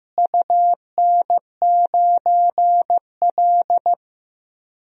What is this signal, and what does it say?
Morse code, heard as a keyed tone, about 15 wpm, 700 Hz. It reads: UN9L
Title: DXCC-100 Call Signs - 4 Characters - Repeated 3x - 15wpm